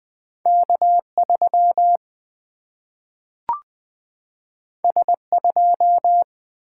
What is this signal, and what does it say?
Morse code, heard as a keyed tone, about 20 wpm, 700 Hz. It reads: K3 S2
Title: Single-Letter Single-Number 20wpm